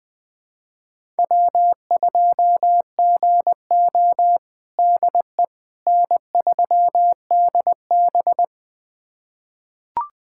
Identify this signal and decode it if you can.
Morse code, heard as a keyed tone, about 20 wpm, 700 Hz. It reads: W2GO DE N3DB
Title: USA Callsign DE Callsign - 4 Characters - 20wpm